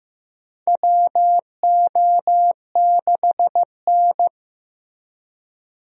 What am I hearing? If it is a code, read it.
Morse code, heard as a keyed tone, about 16 wpm, 700 Hz. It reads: WO6N
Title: USA Call Signs - 4 Characters - Repeated 2x - 15wpm